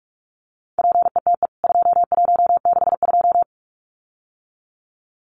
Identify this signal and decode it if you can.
Morse code, heard as a keyed tone, about 44 wpm, 700 Hz. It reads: PETE 2162
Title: CWT Exchange 45wpm